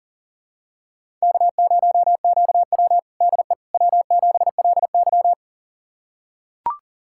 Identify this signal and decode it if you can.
Morse code, heard as a keyed tone, about 40 wpm, 700 Hz. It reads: K0QW DE W7LY
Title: USA Callsign DE Callsign - 4 Characters - 40wpm